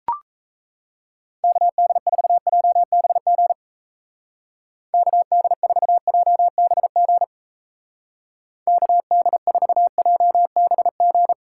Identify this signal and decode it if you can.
Morse code, heard as a keyed tone, about 35 wpm, 700 Hz. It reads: KD4JBG KD4JBG KD4JBG
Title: Speed Racing - USA 2x3 Call Signs 28wpm